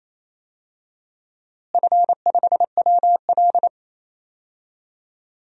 Morse code, heard as a keyed tone, about 28 wpm, 700 Hz. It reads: F5WL
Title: DXCC-100 Call Signs - 4 Characters - Repeated 3x - 28wpm